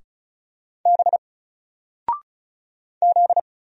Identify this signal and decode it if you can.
Morse code, heard as a keyed tone, about 36 wpm, 700 Hz. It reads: B Z
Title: Single Letter-Number 35wpm